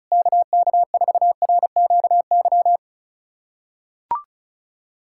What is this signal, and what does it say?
Morse code, heard as a keyed tone, about 34 wpm, 700 Hz. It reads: KK4RQY
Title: USA Callsign DE Callsign 35wpm